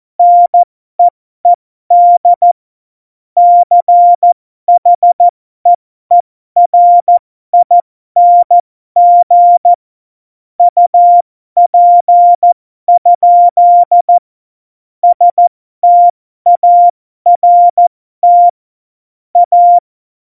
Morse code, heard as a keyed tone, about 14 wpm, spaced as for 12 wpm, 700 Hz. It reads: NEED CHEERING UP? START A
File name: Morsekurs / Witze